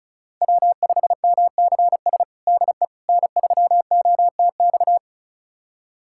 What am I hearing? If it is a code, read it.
Morse code, heard as a keyed tone, about 36 wpm, 700 Hz. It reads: W5MCS DE N3OTX
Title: USA Callsign DE Callsign - 5 Characters - 35wpm